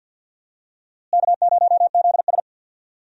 Morse code, heard as a keyed tone, about 47 wpm, 690 Hz. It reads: K0ZS
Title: USA Call Signs - 4 Characters - Repeated 2x - 50wpm